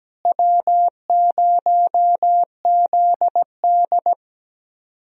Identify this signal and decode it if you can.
Morse code, heard as a keyed tone, about 17 wpm, 700 Hz. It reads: W0ZD
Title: USA Call Signs - 4 Characters - Repeated 2x - 17wpm